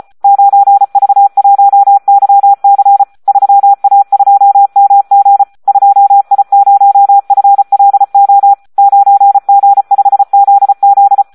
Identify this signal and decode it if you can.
Morse code, heard as a keyed tone, about 34 wpm, 800 Hz. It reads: 9V1YC 3A2MG 2I0FLO 9G5ZZ
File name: Audio practica CW / SD 1 34